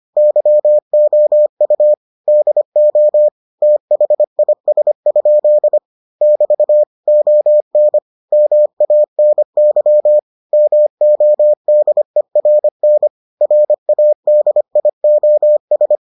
Morse code, heard as a keyed tone, about 25 wpm, 600 Hz. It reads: YOU DO THIS? <BT> ON MANY MODERN RADIOS